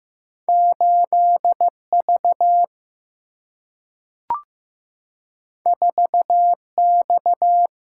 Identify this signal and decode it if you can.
Morse code, heard as a keyed tone, about 15 wpm, 700 Hz. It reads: 8V 4X
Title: Single-Letter Single-Number 15wpm